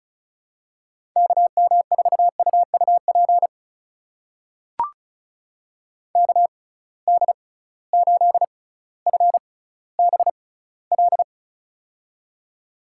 Morse code, heard as a keyed tone, about 36 wpm, spaced as for 18 wpm, 700 Hz. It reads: KM4UUP KD8FBL
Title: USA Call Signs - 6 Characters - 6x Character Spacing - 35wpm